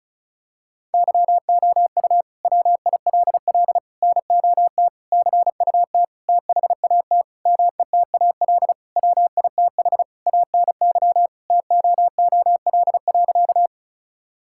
Morse code, heard as a keyed tone, about 34 wpm, 700 Hz. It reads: YOU WILL NOT CUT THAT METAL WITH ANY TOOL.
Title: Sentences from Top 1000 Words - Review Part 6 - 35wpm